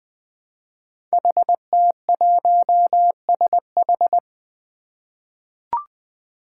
Morse code, heard as a keyed tone, about 20 wpm, 700 Hz. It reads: HT1SH
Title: Five Challenging Letter-Number Combinations 20wpm